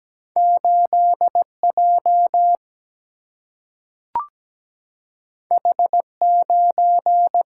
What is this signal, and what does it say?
Morse code, heard as a keyed tone, about 17 wpm, 700 Hz. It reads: 8J H9
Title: Single-Letter Single-Number 17wpm